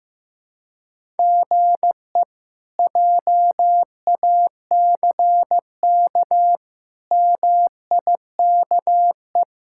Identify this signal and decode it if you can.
Morse code, heard as a keyed tone, about 15 wpm, 700 Hz. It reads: GE JACK MIKE
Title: K1USN SST QSO Practice - 15wpm